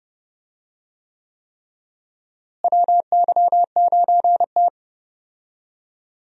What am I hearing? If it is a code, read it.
Morse code, heard as a keyed tone, about 32 wpm, 700 Hz. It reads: WY9T
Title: USA Call Signs - 4 Characters - Repeated 3x - 30wpm